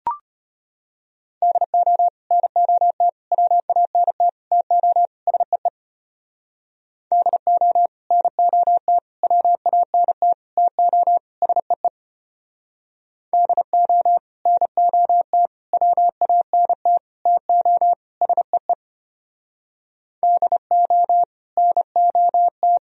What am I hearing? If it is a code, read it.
Morse code, heard as a keyed tone, about 31 wpm, 700 Hz. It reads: DO NOT WANT TO SEE DO NOT WANT TO SEE DO NOT WANT TO SEE DO NOT
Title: Speed-Racing - Sets of 5 Words 25wpm